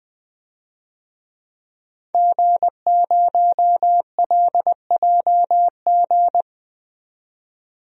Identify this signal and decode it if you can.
Morse code, heard as a keyed tone, about 20 wpm, 700 Hz. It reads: G0LJG
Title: DXCC-100 Call Signs - 5 Characters - 20wpm